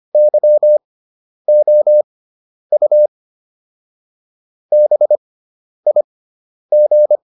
Morse code, heard as a keyed tone, about 25 wpm, spaced as for 10 wpm, 600 Hz. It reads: YOU BIG